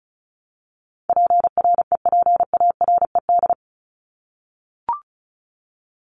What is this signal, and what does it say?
Morse code, heard as a keyed tone, about 34 wpm, 700 Hz. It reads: PREPARED
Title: ED Suffix - Top 100 Words with Base Words 35wpm